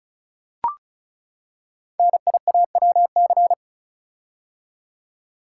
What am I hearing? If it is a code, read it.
Morse code, heard as a keyed tone, about 36 wpm, 690 Hz. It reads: NIAWC
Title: Code Groups - 5 Letters - 35wpm